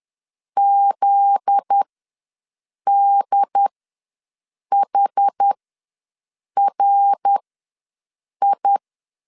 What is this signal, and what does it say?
Morse code, heard as a keyed tone, about 10 wpm, spaced as for 6 wpm, 800 Hz. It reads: ZDHRI
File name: LEZIONI / 10 SIGNS / 10D COMA ws 10wpm 4314